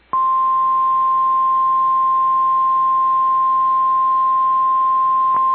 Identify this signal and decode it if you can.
Emergency Alert System (SAME) tones (the 1050 Hz attention tone). It is heard from 0.6 to 5.3 s.
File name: EAS Logs / KMNR